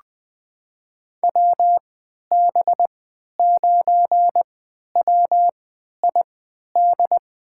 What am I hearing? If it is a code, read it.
Morse code, heard as a keyed tone, about 20 wpm, spaced as for 11 wpm, 700 Hz. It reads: WB9WID
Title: USA Call Signs - 6 Characters - 3x Character Spacing - 20wpm